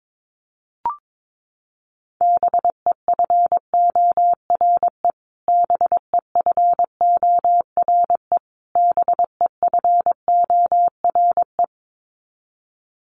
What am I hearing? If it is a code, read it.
Morse code, heard as a keyed tone, about 22 wpm, 700 Hz. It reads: BEFORE BEFORE BEFORE
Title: Top 500 Words Repeated 22wpm